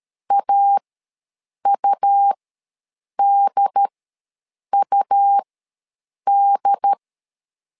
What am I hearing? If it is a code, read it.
Morse code, heard as a keyed tone, about 13 wpm, spaced as for 7 wpm, 800 Hz. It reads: AUDUD
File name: LEZIONI / 2 UHDN / 2 UHD ws 12wpm 1722